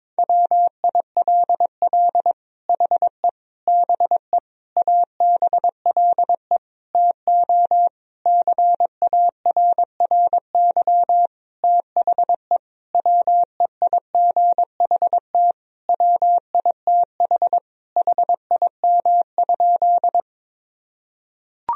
Morse code, heard as a keyed tone, about 23 wpm, 700 Hz. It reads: WILL HE BE ABLE TO CARRY THE WEIGHT WITH HIM?
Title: Sentences from Top 1000 Words - Review Part 3 - 22wpm